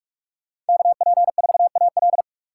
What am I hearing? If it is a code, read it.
Morse code, heard as a keyed tone, about 58 wpm, 700 Hz. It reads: KP4AL